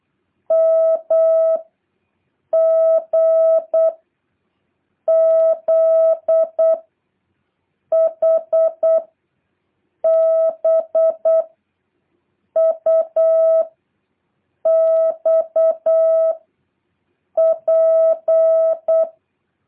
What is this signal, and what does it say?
Morse code, heard as a keyed tone, about 8 wpm, 650 Hz. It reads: M G Z H B U X P
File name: jh8cbh / 2023cw